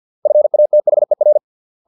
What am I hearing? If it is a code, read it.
Morse code, heard as a keyed tone, about 50 wpm, 600 Hz. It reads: FATHER